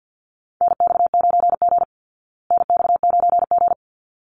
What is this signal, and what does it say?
Morse code, heard as a keyed tone, about 50 wpm, 700 Hz. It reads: NX9G NX9G
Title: Call Signs Repeated 3x - 50wpm